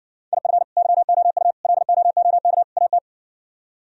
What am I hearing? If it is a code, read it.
Morse code, heard as a keyed tone, about 58 wpm, 700 Hz. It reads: IF YOU LOOK AT